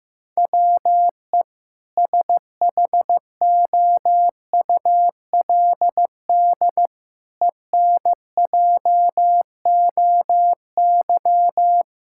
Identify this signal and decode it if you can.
Morse code, heard as a keyed tone, about 16 wpm, 700 Hz. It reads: WE SHOULD ENJOY
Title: Sentences from Top 1000 Words - Review Part 5 - 15wpm